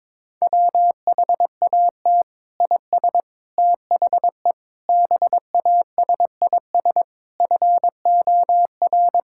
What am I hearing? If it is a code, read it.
Morse code, heard as a keyed tone, about 21 wpm, 700 Hz. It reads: WHAT IS THE BASIS FOR
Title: Sentences from Top 1000 Words - Review Part 6 - 22wpm